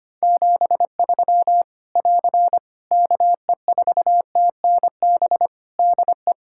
Morse code, heard as a keyed tone, about 25 wpm, 700 Hz. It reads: 73 <AR> KE4TNB DE
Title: Practice QSO - Exchange 4 - Ending - 25wpm